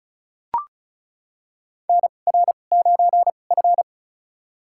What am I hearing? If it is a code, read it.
Morse code, heard as a keyed tone, about 34 wpm, spaced as for 25 wpm, 700 Hz. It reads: NR9F
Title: USA Call Signs - 4 Characters - 2x Character Spacing - 35wpm